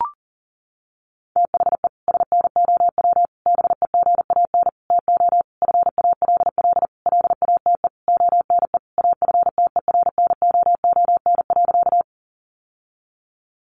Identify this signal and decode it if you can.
Morse code, heard as a keyed tone, about 40 wpm, 700 Hz. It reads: THE SNOW BEGAN TO FALL LATE ONE AFTERNOON.
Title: Sentences from Top 1500 Words 40wpm